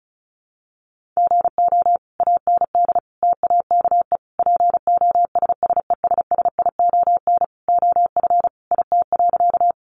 Morse code, heard as a keyed tone, about 34 wpm, 700 Hz. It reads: GO AND TAKE POSSESSION OF IT.
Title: Sentences from Top 1400 Words 35wpm